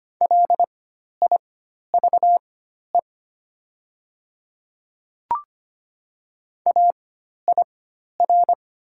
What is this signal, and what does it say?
Morse code, heard as a keyed tone, about 25 wpm, spaced as for 12 wpm, 700 Hz. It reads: LIVE AIR
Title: 4x Character Spacing - Top 300 Words - 25wpm